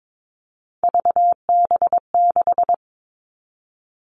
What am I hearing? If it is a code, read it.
Morse code, heard as a keyed tone, about 22 wpm, 700 Hz. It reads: VB6
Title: All Cut Numbers - 3-Digit 22wpm